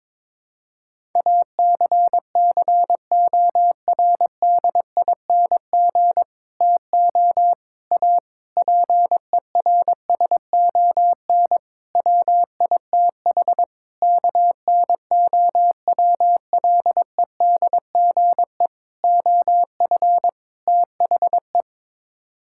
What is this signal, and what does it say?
Morse code, heard as a keyed tone, about 22 wpm, 700 Hz. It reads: ACCORDING TO A PERSON WITH KNOWLEDGE OF THE
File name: practice / 22wpm / Sets of 8 Words - 22wpm